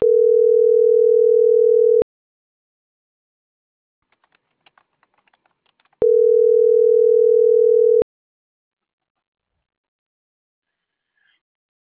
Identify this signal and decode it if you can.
Ringback tone (North America), 440 + 480 Hz.